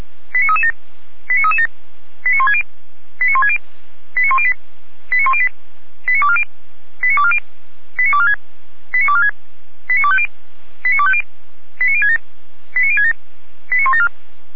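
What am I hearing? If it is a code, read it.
Five-tone selective call (ZVEI), heard as quick runs of five tones, tones of about 70 ms. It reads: 89337 89337 89160 89160 89108 89108 89240 89240 89246 89246 89250 89250 89067 89067 89174
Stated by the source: Five-tone ZVEI selective call 89075 received by SDR.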